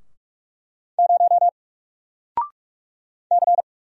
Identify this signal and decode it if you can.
Morse code, heard as a keyed tone, about 44 wpm, 700 Hz. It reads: 0 C
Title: Single Letter-Number Alt 45wpm